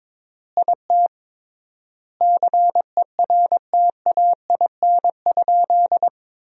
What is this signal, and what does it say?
Morse code, heard as a keyed tone, about 22 wpm, 700 Hz. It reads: IT CERTAIN?
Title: 3x Word-Spacing - Sentences from Top 500 Words - 22wpm